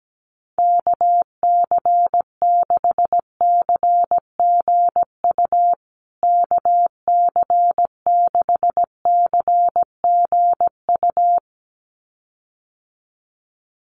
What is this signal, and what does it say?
Morse code, heard as a keyed tone, about 17 wpm, 700 Hz. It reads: KC6CGU KC6CGU
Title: USA Call Signs Repeated 2x 17wpm